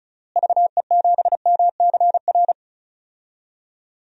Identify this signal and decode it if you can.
Morse code, heard as a keyed tone, about 36 wpm, 700 Hz. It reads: VE7MCR
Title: DXCC-100 Call Signs - 6 Characters - Repeated 3x - 35wpm